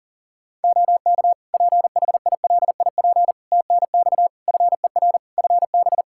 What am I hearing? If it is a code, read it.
Morse code, heard as a keyed tone, about 40 wpm, 700 Hz. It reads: OK PHILIP TNX FER FB
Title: K1USN SST QSO Practice 40wpm